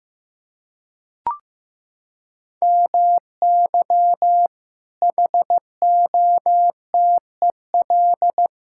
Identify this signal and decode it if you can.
Morse code, heard as a keyed tone, about 15 wpm, 700 Hz. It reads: MY HOTEL
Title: Sentences from Top 1600 Words 15wpm